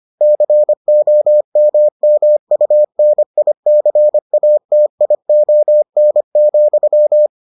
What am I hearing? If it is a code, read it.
Morse code, heard as a keyed tone, about 25 wpm, 600 Hz. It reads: COMMUNICATION,